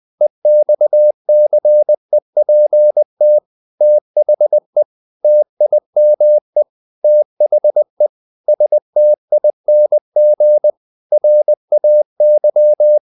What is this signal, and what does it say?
Morse code, heard as a keyed tone, about 20 wpm, 600 Hz. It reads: EXCEPT THE TIME THE STING RAY
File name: the old man and the sea part 2 20wpm